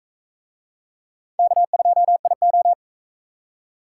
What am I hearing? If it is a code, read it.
Morse code, heard as a keyed tone, about 39 wpm, 700 Hz. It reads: K2IO